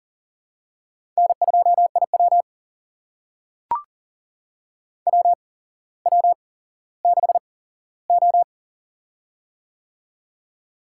Morse code, heard as a keyed tone, about 40 wpm, spaced as for 17 wpm, 700 Hz. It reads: N2IW WW6O
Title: USA Call Signs - 4 Characters - 8x Character Spacing - 40wpm